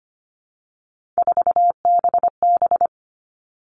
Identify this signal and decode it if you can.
Morse code, heard as a keyed tone, about 25 wpm, 700 Hz. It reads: 4BB
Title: All Cut Numbers - 3-Digit 25wpm